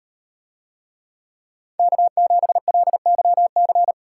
Morse code, heard as a keyed tone, about 36 wpm, 700 Hz. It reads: K7LYC